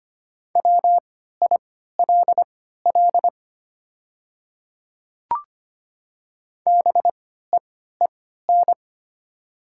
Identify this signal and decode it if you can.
Morse code, heard as a keyed tone, about 25 wpm, spaced as for 14 wpm, 700 Hz. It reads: WILL BEEN
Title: 3x Character Spacing - Top 100 Words - 25wpm